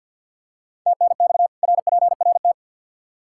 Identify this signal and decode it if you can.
Morse code, heard as a keyed tone, about 52 wpm, 700 Hz. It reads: TNX RPRT
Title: Practice QSO - Exchange 2 - Info - 50wpm